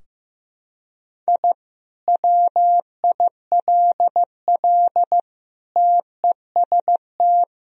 Morse code, heard as a keyed tone, about 15 wpm, 700 Hz. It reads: I WILL TEST
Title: Sentences from Top 1000 Words - Review Part 3 - 15wpm